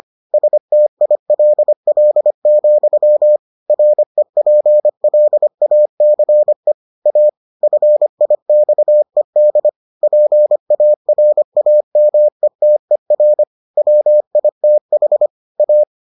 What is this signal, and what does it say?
Morse code, heard as a keyed tone, about 25 wpm, 600 Hz. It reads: STILL, REPLACE A FIXED PARAMETER WITH A